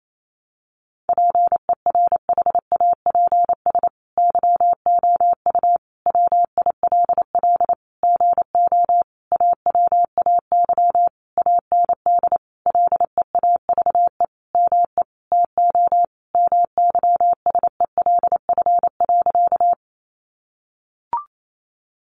Morse code, heard as a keyed tone, about 28 wpm, 700 Hz. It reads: PERHAPS YOU WILL GO AWAY AND LEAVE ME TO MYSELF.
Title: Sentences from Top 1300 Words 28wpm